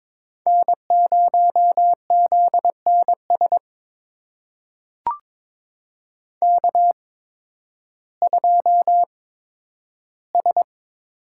Morse code, heard as a keyed tone, about 22 wpm, spaced as for 12 wpm, 700 Hz. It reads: N0ZNS K2S
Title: USA Call Signs - 5 Characters - 8x Character Spacing - 22wpm